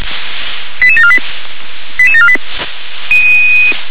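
Five-tone selective call (ZVEI), heard as quick runs of five tones, tones of about 70 ms. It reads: 88748 88748
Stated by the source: ZVEI five-tone call received by SDR.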